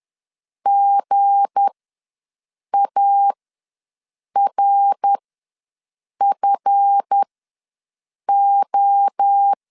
Morse code, heard as a keyed tone, about 10 wpm, spaced as for 6 wpm, 800 Hz. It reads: GARFO
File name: LEZIONI / 9 28 ws 10wpm 4124